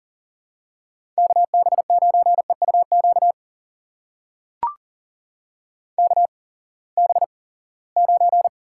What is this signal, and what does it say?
Morse code, heard as a keyed tone, about 40 wpm, spaced as for 23 wpm, 700 Hz. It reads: KB9EUQ KB9
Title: USA Call Signs - 6 Characters - 8x Character Spacing - 40wpm